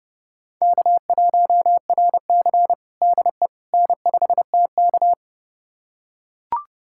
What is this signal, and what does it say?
Morse code, heard as a keyed tone, about 30 wpm, 700 Hz. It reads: K1RC DE N5TK
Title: USA Callsign DE Callsign - 4 Characters - 30wpm